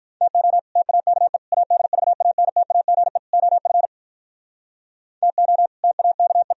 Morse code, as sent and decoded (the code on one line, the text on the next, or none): - --- / - .- -.- . / .- -.. ...- .- -. - .- --. / --- ..-. / - --- / - .- -.- .
TO TAKE ADVANTAGE OF TO TAKE